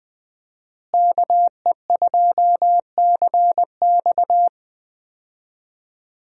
Morse code, heard as a keyed tone, about 20 wpm, 700 Hz. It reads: KE2CX
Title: USA Call Signs - 5 Characters - Repeated 2x - 20wpm